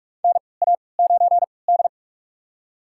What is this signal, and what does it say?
Morse code, heard as a keyed tone, about 45 wpm, spaced as for 25 wpm, 700 Hz. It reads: NA9D
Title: USA Call Signs - 4 Characters - 3x Character Spacing - 45wpm